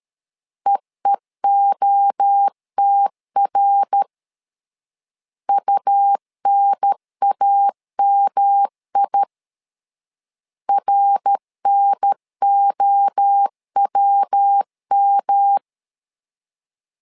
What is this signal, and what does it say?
Morse code, heard as a keyed tone, about 13 wpm, 800 Hz. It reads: EEOTR UNAMI RNOWM
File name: LEZIONI / 3 MWRO / 3 MWRO 12wpm 1730